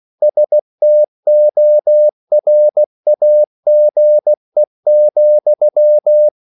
Morse code, heard as a keyed tone, about 16 wpm, 600 Hz. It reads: STORAGE,